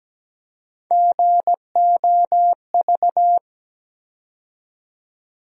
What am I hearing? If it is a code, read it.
Morse code, heard as a keyed tone, about 17 wpm, 700 Hz. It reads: GOV
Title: USA Call Sign Postfixes - 3 Letters 17wpm